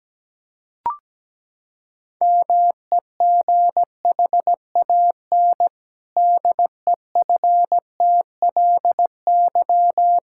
Morse code, heard as a keyed tone, about 17 wpm, 700 Hz. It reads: MEGHAN DEFTLY
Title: Pangram Challenge v2 - 17wpm